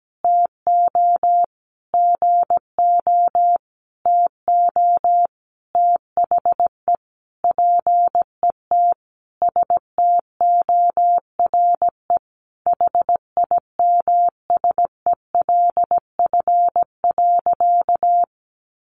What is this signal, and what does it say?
Morse code, heard as a keyed tone, about 17 wpm, 700 Hz. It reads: TO GO TO THE PET STORE HIMSELF.
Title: Easy Phrases 17wpm